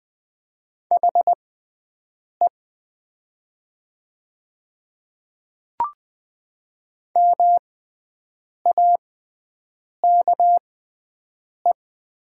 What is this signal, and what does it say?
Morse code, heard as a keyed tone, about 20 wpm, spaced as for 7 wpm, 700 Hz. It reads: HE MAKE
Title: 6x Character Spacing - Top 200 Words - 20wpm